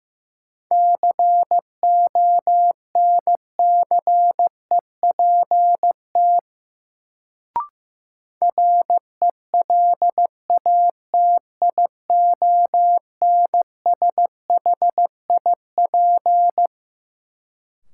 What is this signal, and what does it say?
Morse code, heard as a keyed tone, about 15 wpm, 700 Hz. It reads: CONCEPT RELATIONSHIP
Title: Sentences from Top 1100 Words - New Word Drill - 15wpm